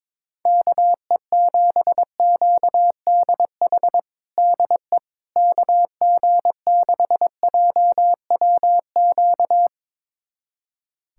Morse code, heard as a keyed tone, about 21 wpm, 700 Hz. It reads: KE7QDH DE KG6JWQ
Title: USA Callsign DE Callsign - 6 Characters - 22wpm